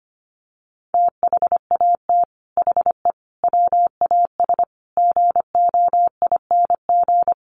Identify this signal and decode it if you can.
Morse code, heard as a keyed tone, about 25 wpm, 700 Hz. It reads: THAT HE WAS GOING
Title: Sets of 5 Words 25wpm